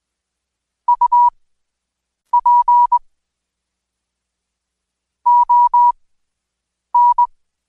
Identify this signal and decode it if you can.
Morse code, heard as a keyed tone, about 21 wpm, spaced as for 7 wpm, 970 Hz. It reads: UP ON
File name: Code Quick / AudioDownld2 / Seg113-Day18-DayDrill 18